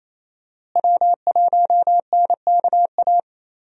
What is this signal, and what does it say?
Morse code, heard as a keyed tone, about 28 wpm, 700 Hz. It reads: W1NKA
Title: USA Call Signs - 5 Characters - Repeated 3x - 28wpm